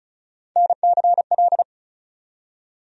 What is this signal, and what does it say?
Morse code, heard as a keyed tone, about 34 wpm, 700 Hz. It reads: NCL
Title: USA Call Sign Postfixes - 3 Letters 35wpm